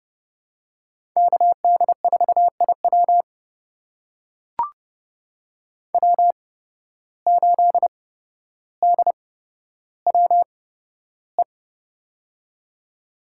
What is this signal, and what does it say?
Morse code, heard as a keyed tone, about 32 wpm, spaced as for 13 wpm, 700 Hz. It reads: KD4IW W8DWE
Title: USA Call Signs - 5 Characters - 8x Character Spacing - 30wpm